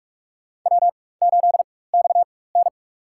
Morse code, heard as a keyed tone, about 46 wpm, spaced as for 21 wpm, 700 Hz. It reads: W8XN
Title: USA Call Signs - 4 Characters - 4x Character Spacing - 45wpm